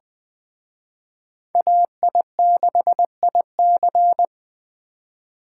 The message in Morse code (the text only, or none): AI6IC